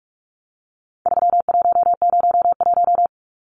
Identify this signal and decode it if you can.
Morse code, heard as a keyed tone, about 44 wpm, 700 Hz. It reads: 3101
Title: Numbers - 4-Digit 45wpm